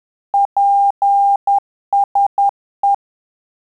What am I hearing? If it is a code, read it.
Morse code, heard as a keyed tone, about 10 wpm, 800 Hz. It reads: PSE
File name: Telegraphy-group / QSO CW 50 CPM